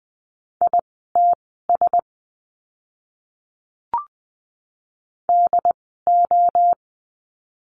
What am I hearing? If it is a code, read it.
Morse code, heard as a keyed tone, about 20 wpm, spaced as for 14 wpm, 700 Hz. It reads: ITS DO
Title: 108 Lesson 18f - Letter m - Words 20wpm